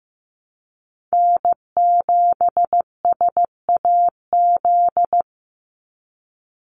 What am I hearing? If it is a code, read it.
Morse code, heard as a keyed tone, about 15 wpm, 700 Hz. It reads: N7SAZ
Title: USA Call Signs 15wpm